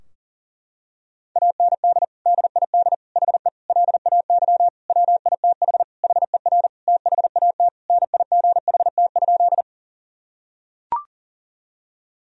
Morse code, heard as a keyed tone, about 40 wpm, 700 Hz. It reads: AND DID HE LAY WITH HER THAT NIGHT?
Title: Sentences from Top 1000 Words - Review Part 2 - 40wpm